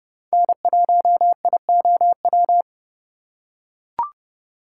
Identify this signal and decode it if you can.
Morse code, heard as a keyed tone, about 30 wpm, 700 Hz. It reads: N1IOW
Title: USA Call Signs - 5 Characters - 3x Character Spacing - 30wpm